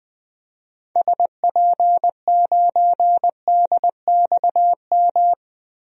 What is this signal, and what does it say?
Morse code, heard as a keyed tone, about 20 wpm, 700 Hz. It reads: SP9DXM
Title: DXCC-100 Call Signs - 6 Characters - 20wpm